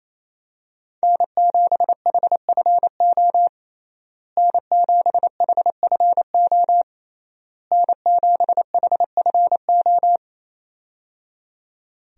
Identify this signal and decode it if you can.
Morse code, heard as a keyed tone, about 28 wpm, 700 Hz. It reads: N7HFO N7HFO N7HFO
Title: USA Call Signs - 5 Characters - Repeated 3x - 28wpm